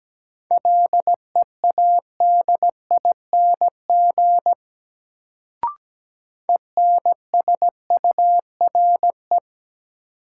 Morse code, heard as a keyed tone, about 17 wpm, 700 Hz. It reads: LEADING ENSURE
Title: Sentences from Top 1100 Words - New Word Drill - 17wpm